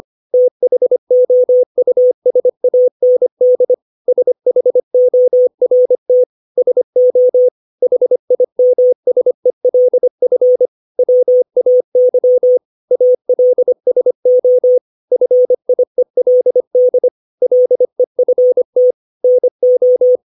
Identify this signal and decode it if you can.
Morse code, heard as a keyed tone, about 25 wpm, 500 Hz. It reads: THOUSAND SHORT SO HIMSELF WAY ALSO FIELD LEFT NO
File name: 632 words / 632 Words 25WPM